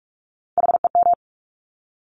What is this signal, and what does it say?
Morse code, heard as a keyed tone, about 46 wpm, 700 Hz. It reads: HEM
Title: Top Three Letter Combinations in Words 45wpm